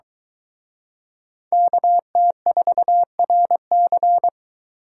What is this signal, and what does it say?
Morse code, heard as a keyed tone, about 23 wpm, 700 Hz. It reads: KT4RC